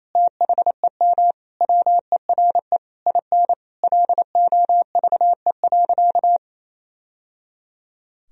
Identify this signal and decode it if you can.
Morse code, heard as a keyed tone, about 28 wpm, 700 Hz. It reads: THEM WERE IN LOVE.
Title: Sentences from Top 1000 Words - Review Part 4 - 28wpm